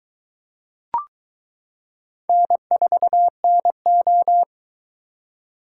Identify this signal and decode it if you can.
Morse code, heard as a keyed tone, about 22 wpm, 700 Hz. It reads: N4NO